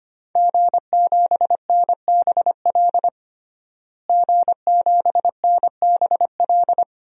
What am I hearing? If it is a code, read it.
Morse code, heard as a keyed tone, about 25 wpm, 700 Hz. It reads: G7NBL G7NBL
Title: DXCC-100 Call Signs - 5 Characters - Repeated 2x - 25wpm